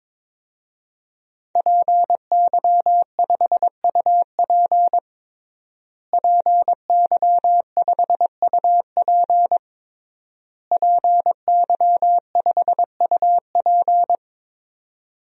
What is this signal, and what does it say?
Morse code, heard as a keyed tone, about 22 wpm, 700 Hz. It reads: PY5UP PY5UP PY5UP
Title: DXCC-100 Call Signs - 5 Characters - Repeated 3x - 22wpm